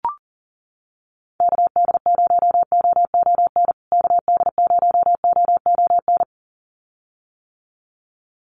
Morse code, heard as a keyed tone, about 40 wpm, 700 Hz. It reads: KD0OON KD0OON
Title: USA Call Signs Repeated 2x 40wpm